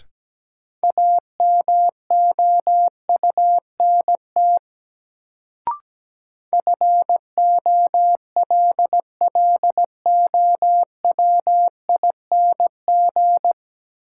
Morse code, heard as a keyed tone, about 17 wpm, 700 Hz. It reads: AMOUNT FOLLOWING
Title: Sentences from Top 700 Words - New Word Drill - 17wpm